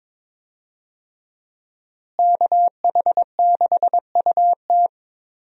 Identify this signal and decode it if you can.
Morse code, heard as a keyed tone, about 21 wpm, 700 Hz. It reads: KH6UT
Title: USA Call Signs - 5 Characters - Repeated 2x - 22wpm